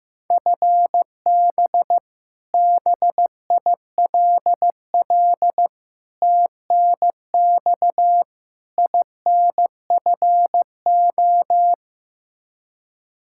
Morse code, heard as a keyed tone, about 15 wpm, 700 Hz. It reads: FB BILL TNX INFO
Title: Practice QSO - Exchange 3 - More Info - 15wpm